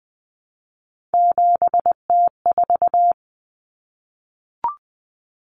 Morse code, heard as a keyed tone, about 20 wpm, 700 Hz. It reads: 7T4
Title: Common Cut Numbers - 3-Digit 20wpm